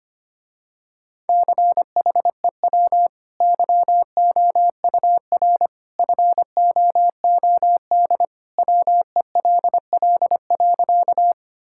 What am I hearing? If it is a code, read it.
Morse code, heard as a keyed tone, about 25 wpm, 700 Hz. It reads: CHEW YOUR FOOD WELL.
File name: Easy Sentences with Hints - 25wpm